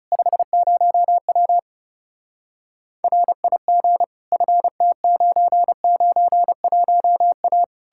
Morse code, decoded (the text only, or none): 50W RIG FT991A